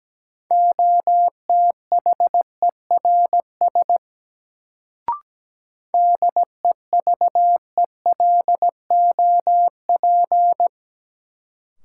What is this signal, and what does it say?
Morse code, heard as a keyed tone, about 17 wpm, 700 Hz. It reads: OTHERS DEVELOP
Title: Sentences from Top 1000 Words - Review Part 2 - New Word Drill - 17wpm